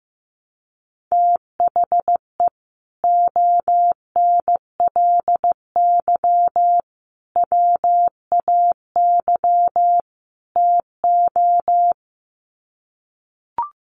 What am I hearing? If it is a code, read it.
Morse code, heard as a keyed tone, about 15 wpm, 700 Hz. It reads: THE ONLY WAY TO
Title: Sets of 4 Words 15wpm